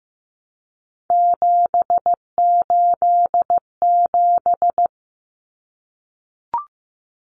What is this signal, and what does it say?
Morse code, heard as a keyed tone, about 15 wpm, 700 Hz. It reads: 787
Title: Common Cut Numbers - 3-Digit 15wpm